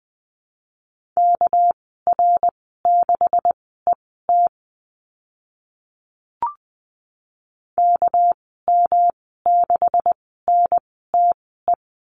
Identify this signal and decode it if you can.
Morse code, heard as a keyed tone, about 20 wpm, spaced as for 14 wpm, 700 Hz. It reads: KR6ET KM6NTE
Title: 231 Lesson 37g - Letter x - Call Signs 20wpm